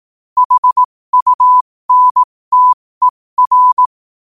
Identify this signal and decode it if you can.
Morse code, heard as a keyed tone, about 18 wpm, spaced as for 15 wpm, 1000 Hz. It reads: HUNTER
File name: qotdfiles / 15wpm